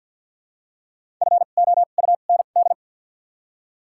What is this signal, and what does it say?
Morse code, heard as a keyed tone, about 53 wpm, 700 Hz. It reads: F O U N D
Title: 2x Character Spacing - Top 500 Words - 50wpm